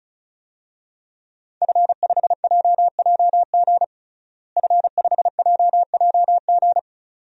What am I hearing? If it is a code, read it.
Morse code, heard as a keyed tone, about 36 wpm, 700 Hz. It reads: F5JJG F5JJG
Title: DXCC-100 Call Signs - 5 Characters - Repeated 2x - 35wpm